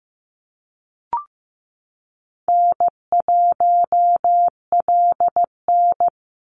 Morse code, heard as a keyed tone, about 15 wpm, 700 Hz. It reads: N1LN
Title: Call Signs Repeated 3x - 15wpm